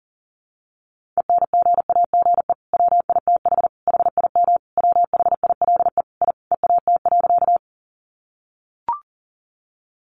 Morse code, heard as a keyed tone, about 40 wpm, 700 Hz. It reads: ENGAGE WITH HIM WHILE I EAT.
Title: Sentences from Top 1500 Words 40wpm